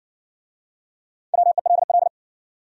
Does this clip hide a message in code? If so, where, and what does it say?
Yes: Morse code from 1.3 to 2.1 s, heard as a keyed tone, about 57 wpm, 700 Hz. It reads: WELL